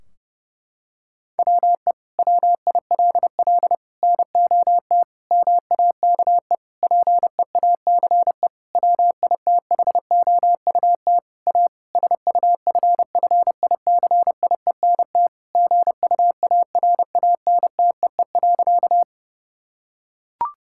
Morse code, heard as a keyed tone, about 30 wpm, 700 Hz. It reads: WE WILL NOT MAKE PEACE WITHOUT A SUFFICIENT GUARANTEE.
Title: Sentences from Top 1600 Words 30wpm